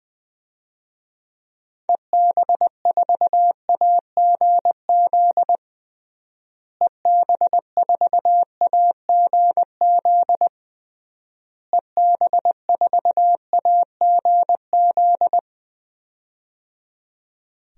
Morse code, heard as a keyed tone, about 20 wpm, 700 Hz. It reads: EB4AGZ EB4AGZ EB4AGZ
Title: DXCC-100 Call Signs - 6 Characters - Repeated 3x - 20wpm